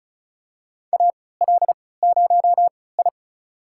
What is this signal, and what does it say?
Morse code, heard as a keyed tone, about 34 wpm, spaced as for 20 wpm, 690 Hz. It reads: AL0I
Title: USA Call Signs - 3x Character Spacing - 35wpm